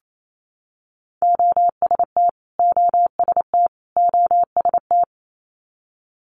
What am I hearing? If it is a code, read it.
Morse code, heard as a keyed tone, about 28 wpm, 700 Hz. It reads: OST OST OST
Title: Multi-letter Phonemes Repeated 3x 28wpm